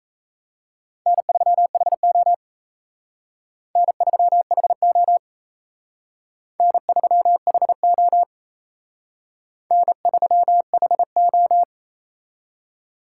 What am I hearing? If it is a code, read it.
Morse code, heard as a keyed tone, about 35 wpm, 700 Hz. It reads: N3HO N3HO N3HO N3HO
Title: Speed Racing - USA 1x2 Call Signs 28wpm